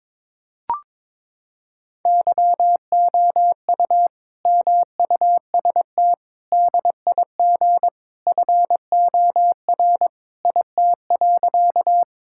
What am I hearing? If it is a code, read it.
Morse code, heard as a keyed tone, about 22 wpm, 700 Hz. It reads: YOU MUST DIG FOR IT.
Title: Sentences from Top 1600 Words 22wpm